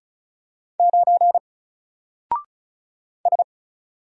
Morse code, heard as a keyed tone, about 33 wpm, 700 Hz. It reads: 9 S
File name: Single-Letter-Number-35wpm